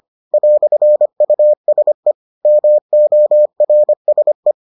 Morse code, heard as a keyed tone, about 25 wpm, 600 Hz. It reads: "USE MORSE